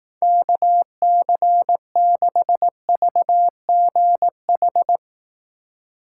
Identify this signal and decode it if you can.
Morse code, heard as a keyed tone, about 19 wpm, 700 Hz. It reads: KC6VGH
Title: Speed Racing - USA Call Signs 15wpm